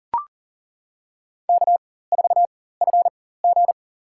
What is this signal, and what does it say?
Morse code, heard as a keyed tone, about 40 wpm, spaced as for 19 wpm, 700 Hz. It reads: K4FG
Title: USA Call Signs - 4x Character Spacing - 40wpm